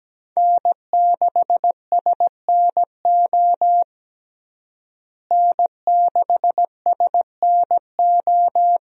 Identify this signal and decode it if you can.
Morse code, heard as a keyed tone, about 17 wpm, 700 Hz. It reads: N6SNO N6SNO
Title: USA Call Signs - 5 Characters - Repeated 2x - 17wpm